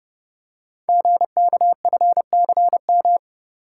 Morse code, heard as a keyed tone, about 30 wpm, 700 Hz. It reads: GKFCM
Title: Code Groups - 5 Letters - 30wpm